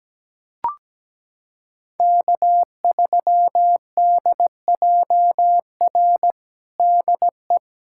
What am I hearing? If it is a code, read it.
Morse code, heard as a keyed tone, about 17 wpm, 700 Hz. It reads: K3DJR DE
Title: USA Callsign DE Callsign - 5 Characters - 17wpm